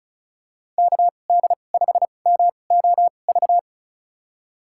Morse code, heard as a keyed tone, about 34 wpm, spaced as for 25 wpm, 700 Hz. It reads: KD5MOV
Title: USA Call Signs - 6 Characters - 2x Character Spacing - 35wpm